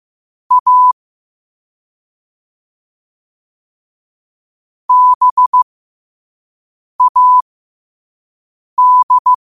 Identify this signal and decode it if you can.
Morse code, heard as a keyed tone, about 15 wpm, spaced as for 5 wpm, 1000 Hz. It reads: A BAD
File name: qotdfiles / 5wpm